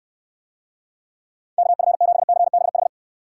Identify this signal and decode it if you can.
Morse code, heard as a keyed tone, about 68 wpm, 700 Hz. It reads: KF7CBH